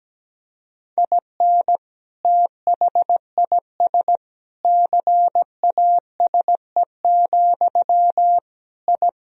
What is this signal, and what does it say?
Morse code, heard as a keyed tone, about 17 wpm, 700 Hz. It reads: IN THIS CASE, I
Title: Sentences from Top 1000 Words - Review Part 4 - 17wpm